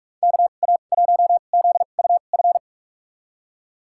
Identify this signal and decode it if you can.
Morse code, heard as a keyed tone, about 43 wpm, spaced as for 32 wpm, 700 Hz. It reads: KA1ZUF
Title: USA Call Signs - 6 Characters - 2x Character Spacing - 45wpm